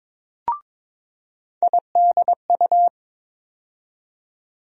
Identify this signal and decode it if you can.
Morse code, heard as a keyed tone, about 22 wpm, 700 Hz. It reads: IDU
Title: USA Call Sign Postfixes - 3 Letters 22wpm